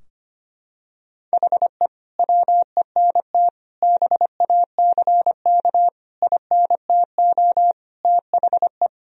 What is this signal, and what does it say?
Morse code, heard as a keyed tone, about 25 wpm, 700 Hz. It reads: HE WENT BACK INTO THE
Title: Sentences from Top 1000 Words - Review Part 2 - 25wpm